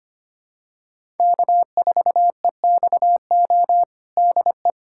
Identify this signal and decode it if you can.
Morse code, heard as a keyed tone, about 25 wpm, 700 Hz. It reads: K4EXO DE
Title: USA Callsign DE Callsign 25wpm